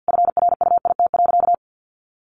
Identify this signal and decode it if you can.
Morse code, heard as a keyed tone, about 50 wpm, 700 Hz. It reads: FRUIT.
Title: Sentences from Top 1300 Words 50wpm